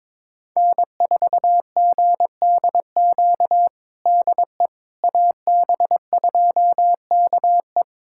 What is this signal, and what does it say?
Morse code, heard as a keyed tone, about 22 wpm, 700 Hz. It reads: N4GDQ DE AB2KE
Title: USA Callsign DE Callsign - 5 Characters - 22wpm